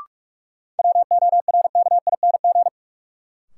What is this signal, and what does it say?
Morse code, heard as a keyed tone, about 44 wpm, 700 Hz. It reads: WORKING
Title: Sentences from Top 700 Words - New Word Drill - 45wpm